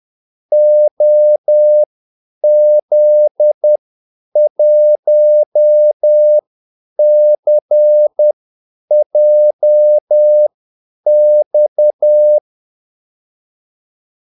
Morse code, heard as a keyed tone, about 10 wpm, spaced as for 8 wpm, 600 Hz. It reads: OZ1CJX
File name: Callsigns-10-8wpm